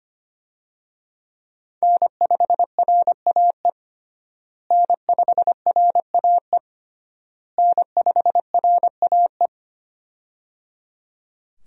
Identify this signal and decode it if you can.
Morse code, heard as a keyed tone, about 25 wpm, 700 Hz. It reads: N5RAE N5RAE N5RAE
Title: USA Call Signs - 5 Characters - Repeated 3x - 25wpm